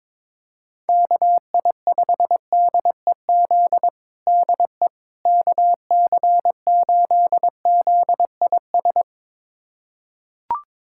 Morse code, heard as a keyed tone, about 22 wpm, 700 Hz. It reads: KI5DEZ DE KC8ZIS
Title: USA Callsign DE Callsign 22wpm